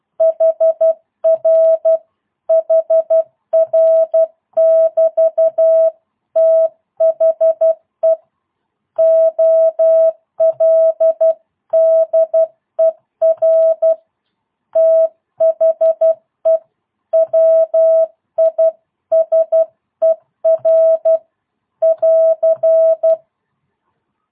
Morse code, heard as a keyed tone, about 12 wpm, 650 Hz. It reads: HR HR <BT> THE OLDER THE WISER <AR>
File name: jh8cbh / 2022cw